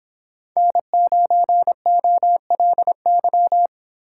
Morse code, heard as a keyed tone, about 27 wpm, 700 Hz. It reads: N9OLY
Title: Speed Racing - USA Call Signs 22wpm